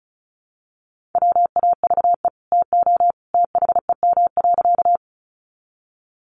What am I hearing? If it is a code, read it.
Morse code, heard as a keyed tone, about 34 wpm, 700 Hz. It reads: WAVE TO THEM.
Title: Sentences from Top 1400 Words 35wpm